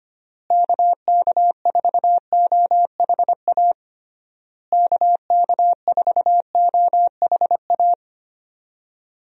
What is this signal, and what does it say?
Morse code, heard as a keyed tone, about 25 wpm, 700 Hz. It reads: KK4OHA KK4OHA
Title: USA Call Signs - 6 Characters - Repeated 3x - 25wpm